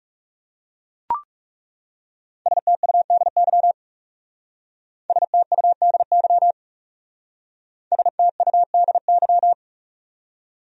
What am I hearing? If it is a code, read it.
Morse code, heard as a keyed tone, about 40 wpm, 700 Hz. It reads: STUDY STUDY STUDY